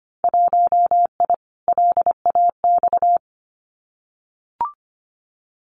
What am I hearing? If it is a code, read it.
Morse code, heard as a keyed tone, about 25 wpm, 700 Hz. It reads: 1I LAX
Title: Winter Field Day Exchanges Encore 25wpm